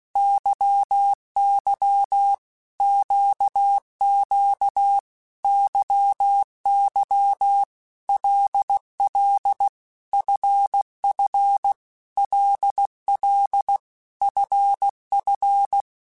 Morse code, heard as a keyed tone, about 16 wpm, 800 Hz. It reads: YY QQ YY LL FF LL FF
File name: Basic Letter Groups By7Groups / LFCYQ / LFCYQ@15